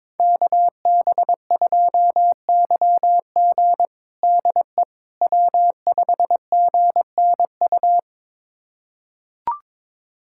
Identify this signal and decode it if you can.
Morse code, heard as a keyed tone, about 21 wpm, 700 Hz. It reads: KB2YG DE W5GNU
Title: USA Callsign DE Callsign - 5 Characters - 22wpm